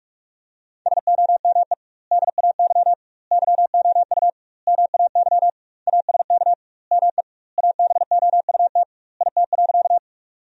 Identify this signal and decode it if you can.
Morse code, heard as a keyed tone, about 41 wpm, 700 Hz. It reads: SOME DAY YOU MAY ASK ME ABOUT IT.
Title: 2x Word-Spacing - Sentences from Top 200 Words - 45wpm